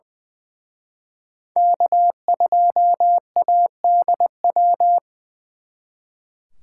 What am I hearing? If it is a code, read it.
Morse code, heard as a keyed tone, about 24 wpm, 700 Hz. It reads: K2ADW
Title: Speed Racing - USA 1x3 Call Signs 20wpm